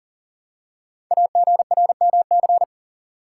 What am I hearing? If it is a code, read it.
Morse code, heard as a keyed tone, about 42 wpm, 700 Hz. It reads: AGRMC
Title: Code Groups - 5 Letters - 40wpm